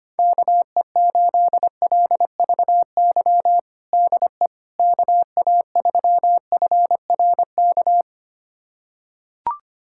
Morse code, heard as a keyed tone, about 25 wpm, 700 Hz. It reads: KE8LVY DE KA3FRK
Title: USA Callsign DE Callsign - 6 Characters - 25wpm